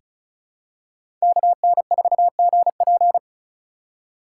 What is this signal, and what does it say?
Morse code, heard as a keyed tone, about 36 wpm, 700 Hz. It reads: KN4GP
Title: USA Call Signs - 3x Character Spacing - 35wpm